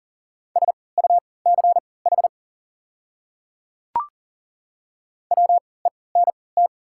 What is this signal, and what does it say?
Morse code, heard as a keyed tone, about 40 wpm, spaced as for 23 wpm, 700 Hz. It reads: SUCH WENT
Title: 3x Character Spacing - Top 200 Words - 40wpm